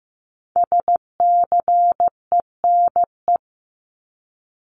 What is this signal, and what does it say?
Morse code, heard as a keyed tone, about 15 wpm, 700 Hz. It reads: SCENE
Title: Top 1001-1250 Words 15wpm